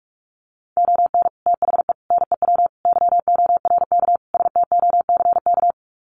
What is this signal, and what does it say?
Morse code, heard as a keyed tone, about 44 wpm, 700 Hz. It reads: ON THE NEW YORK STOCK
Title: Sets of 5 Words Encore 45wpm